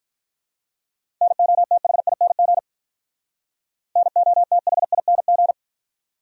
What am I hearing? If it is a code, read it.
Morse code, heard as a keyed tone, about 46 wpm, 700 Hz. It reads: NOTHING NOTHING